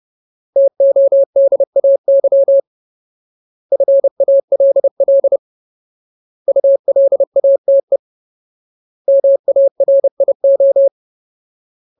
Morse code, heard as a keyed tone, about 30 wpm, 550 Hz. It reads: TODAY FALL ULATE MARIO
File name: Ultimate Word List / 30WPM Ultimate